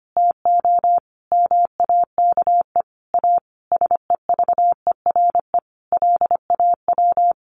Morse code, heard as a keyed tone, about 25 wpm, 700 Hz. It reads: TO MAKE A SEVERE LAW
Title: Sentences from Top 1400 Words 25wpm